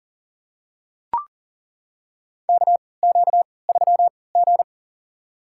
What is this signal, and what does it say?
Morse code, heard as a keyed tone, about 40 wpm, spaced as for 23 wpm, 700 Hz. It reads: KQ3G
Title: USA Call Signs - 4 Characters - 3x Character Spacing - 40wpm